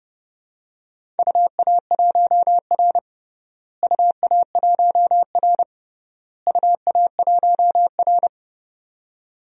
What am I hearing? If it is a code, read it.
Morse code, heard as a keyed tone, about 31 wpm, 700 Hz. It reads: UA1R UA1R UA1R
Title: DXCC-100 Call Signs - 4 Characters - Repeated 3x - 30wpm